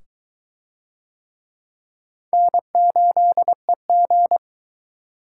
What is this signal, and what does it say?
Morse code, heard as a keyed tone, about 22 wpm, 700 Hz. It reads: N8EG